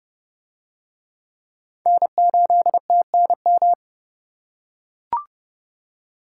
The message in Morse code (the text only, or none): N8TNM